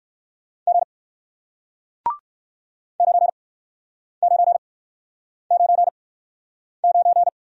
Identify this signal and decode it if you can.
Morse code, heard as a keyed tone, about 60 wpm, spaced as for 9 wpm, 700 Hz. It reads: G 9999